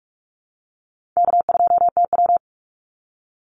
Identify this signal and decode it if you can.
Morse code, heard as a keyed tone, about 44 wpm, 700 Hz. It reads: K2TW
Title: Call Signs - 45wpm